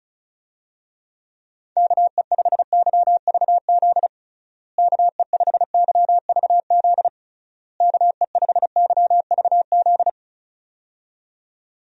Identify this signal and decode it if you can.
Morse code, heard as a keyed tone, about 36 wpm, 700 Hz. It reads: KE5YVZ KE5YVZ KE5YVZ
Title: USA Call Signs - 6 Characters - Repeated 3x - 35wpm